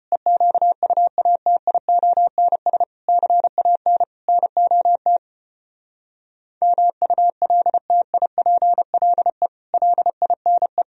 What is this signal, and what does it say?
Morse code, heard as a keyed tone, about 33 wpm, 700 Hz. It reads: EQUATIONS CAN NOT MULTIPLE LINE